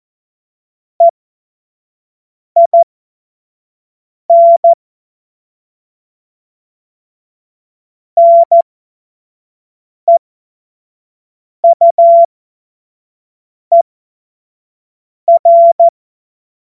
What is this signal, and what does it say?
Morse code, heard as a keyed tone, about 14 wpm, spaced as for 5 wpm, 680 Hz. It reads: EIN NEUER
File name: Morsekurs / Witze / Witz 30. Dezember 2025 25 F70